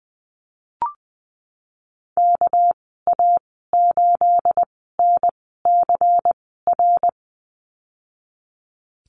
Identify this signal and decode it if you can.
Morse code, heard as a keyed tone, about 20 wpm, spaced as for 15 wpm, 700 Hz. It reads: KA8NCR
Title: 219 Lesson 35g - Number 0 - Call Signs 20wpm